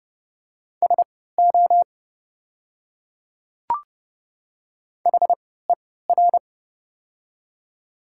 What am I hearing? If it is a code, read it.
Morse code, heard as a keyed tone, about 32 wpm, spaced as for 17 wpm, 700 Hz. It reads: SO HER
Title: 3x Character Spacing - Top 200 Words - 30wpm